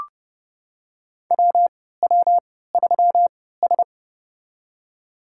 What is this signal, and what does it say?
Morse code, heard as a keyed tone, about 30 wpm, spaced as for 17 wpm, 700 Hz. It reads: WW3S
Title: USA Call Signs - 4 Characters - 3x Character Spacing - 30wpm